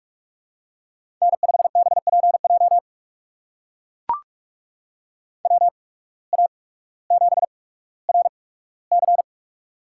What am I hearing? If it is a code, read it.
Morse code, heard as a keyed tone, about 44 wpm, spaced as for 19 wpm, 700 Hz. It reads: N5BPJ WA7RC
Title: USA Call Signs - 5 Characters - 8x Character Spacing - 45wpm